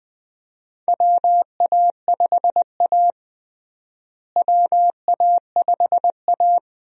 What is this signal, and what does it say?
Morse code, heard as a keyed tone, about 20 wpm, 700 Hz. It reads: WA5A WA5A
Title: USA Call Signs - 4 Characters - Repeated 2x - 20wpm